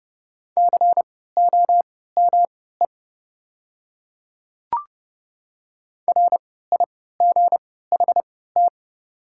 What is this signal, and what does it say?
Morse code, heard as a keyed tone, about 30 wpm, spaced as for 17 wpm, 700 Hz. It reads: COME RIGHT
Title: 3x Character Spacing - Top 200 Words - 30wpm